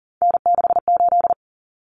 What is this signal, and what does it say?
Morse code, heard as a keyed tone, about 40 wpm, 700 Hz. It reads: N68
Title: Common Cut Numbers - 3-Digit 40wpm